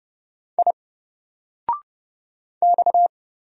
Morse code, heard as a keyed tone, about 30 wpm, 700 Hz. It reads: I X
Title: Single Letter-Number 30wpm